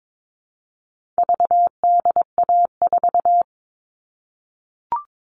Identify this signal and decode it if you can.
Morse code, heard as a keyed tone, about 22 wpm, 700 Hz. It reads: VDA4
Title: All Cut Numbers - 4-Digit 22wpm